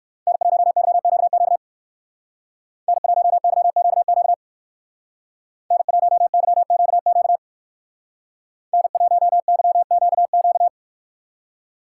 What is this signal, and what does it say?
Morse code, heard as a keyed tone, about 53 wpm, 700 Hz. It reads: N1YQQ N1YQQ N1YQQ N1YQQ
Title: Speed Racing - USA 1x3 Call Signs 45wpm